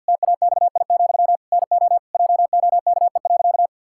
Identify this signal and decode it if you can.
Morse code, heard as a keyed tone, about 50 wpm, 700 Hz. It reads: TAXI, NO JOKE.
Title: Pangram Challenge v2 - 50wpm